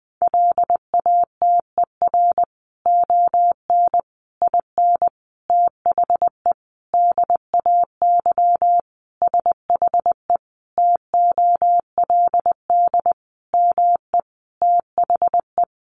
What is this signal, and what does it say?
Morse code, heard as a keyed tone, about 20 wpm, 700 Hz. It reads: LATER ON IN THE DAY SHE TOLD ME THE
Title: Sentences from Top 700 Words 20wpm